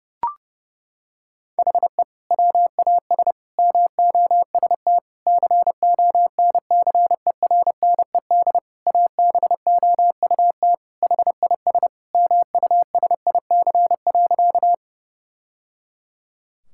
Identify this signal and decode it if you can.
Morse code, heard as a keyed tone, about 30 wpm, 700 Hz. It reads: HE WAS MOST CONCERNED ABOUT HIS MUSIC.
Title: Sentences from Top 1000 Words - Review Part 6 - 30wpm